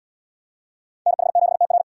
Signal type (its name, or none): Morse code